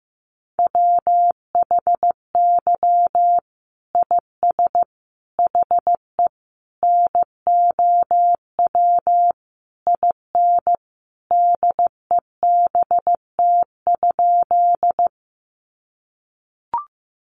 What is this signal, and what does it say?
Morse code, heard as a keyed tone, about 15 wpm, 700 Hz. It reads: WHY IS HE NOW IN DEBT?
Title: Sentences from Top 1300 Words 15wpm